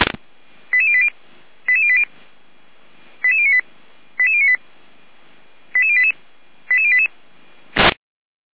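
Five-tone selective call (ZVEI), heard as quick runs of five tones, tones of about 70 ms. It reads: 88080 88080 88098 88098 88088 88088
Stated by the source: Received on SDR (ZVEI).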